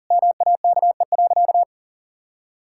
Morse code, heard as a keyed tone, about 40 wpm, 700 Hz. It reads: MAKE.
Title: Sentences from Top 1000 Words - Review Part 5 - 40wpm